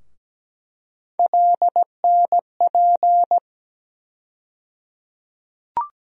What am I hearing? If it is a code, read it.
Morse code, heard as a keyed tone, about 17 wpm, 700 Hz. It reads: LNP
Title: USA Call Sign Postfixes - 3 Letters 17wpm